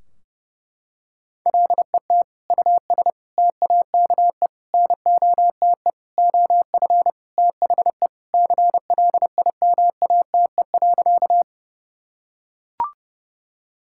Morse code, heard as a keyed tone, about 30 wpm, 700 Hz. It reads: LET US TAKE NOTE OF THE CLIMATE.
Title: Sentences from Top 1600 Words 30wpm